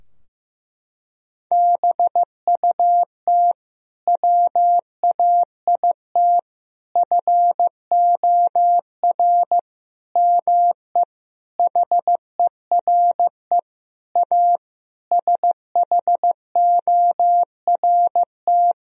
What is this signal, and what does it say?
Morse code, heard as a keyed tone, about 15 wpm, 700 Hz. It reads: BUT WAIT FOR ME HERE A SHORT
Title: Sentences from Top 1000 Words - Review Part 2 - 15wpm